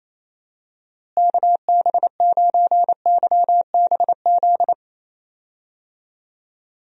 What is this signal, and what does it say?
Morse code, heard as a keyed tone, about 27 wpm, 700 Hz. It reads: KB9YBZ
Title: USA Call Signs - 6 Characters - Repeated 3x - 28wpm